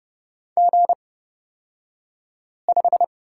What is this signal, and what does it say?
Morse code, heard as a keyed tone, about 29 wpm, spaced as for 5 wpm, 690 Hz. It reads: G5
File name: Lesson 36d - BT - Two Characters 30wpm